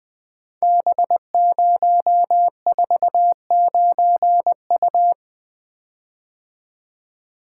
Morse code, heard as a keyed tone, about 20 wpm, 700 Hz. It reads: B049U
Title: Code Groups - 5 Letter:Numbers - 20wpm